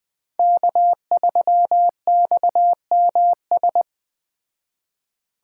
Morse code, heard as a keyed tone, about 21 wpm, 700 Hz. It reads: K3XMS
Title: Speed Racing - USA 1x3 Call Signs 17wpm